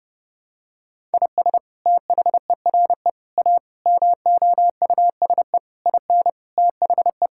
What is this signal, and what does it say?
Morse code, heard as a keyed tone, about 30 wpm, 700 Hz. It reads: IS THERE A MOUSE IN THE
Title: Sentences from Top 1000 Words - Review Part 5 - 30wpm